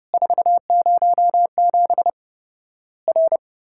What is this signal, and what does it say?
Morse code, heard as a keyed tone, about 30 wpm, 700 Hz. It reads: 407 R
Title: Worked All Europe DX Contest - QTC Exchange - 30wpm